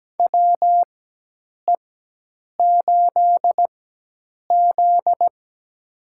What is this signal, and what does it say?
Morse code, heard as a keyed tone, about 17 wpm, spaced as for 8 wpm, 700 Hz. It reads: WE8Z
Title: USA Call Signs - 4x Character Spacing - 17wpm